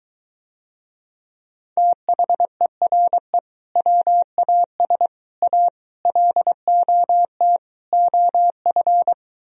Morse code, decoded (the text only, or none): THERE WAS A LOT OF